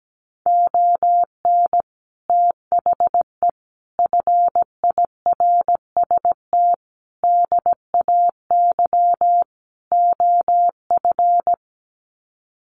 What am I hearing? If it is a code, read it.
Morse code, heard as a keyed tone, about 17 wpm, 700 Hz. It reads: ON THE FIRST DAY OF
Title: Sets of 5 Words Encore 17wpm